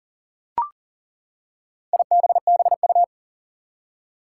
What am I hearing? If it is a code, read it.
Morse code, heard as a keyed tone, about 40 wpm, 700 Hz. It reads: IBBU
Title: Four Challenging Letter-Number Combinations 40wpm